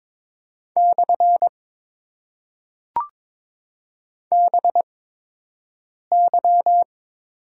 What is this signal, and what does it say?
Morse code, heard as a keyed tone, about 21 wpm, spaced as for 6 wpm, 700 Hz. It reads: / BY
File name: Top 90 QSO Elements - 8x Character Spacing - 22wpm